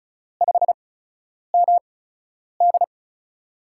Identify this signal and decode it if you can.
Morse code, heard as a keyed tone, about 36 wpm, spaced as for 10 wpm, 700 Hz. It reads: 5MD
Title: USA Call Signs - 4 Characters - 8x Character Spacing - 35wpm